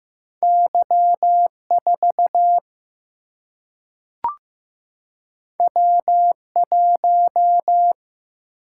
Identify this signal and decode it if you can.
Morse code, heard as a keyed tone, about 15 wpm, 700 Hz. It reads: Y 4 W 1
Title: Single-Letter Single-Number 15wpm